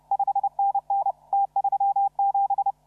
Morse code, heard as a keyed tone, about 34 wpm, 780 Hz. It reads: 5NN T37
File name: CW / RHOB CW 2017 HA6P 0001